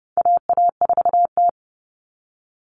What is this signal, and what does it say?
Morse code, heard as a keyed tone, about 30 wpm, 700 Hz. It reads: AA4T
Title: All Cut Numbers - 4-Digit 30wpm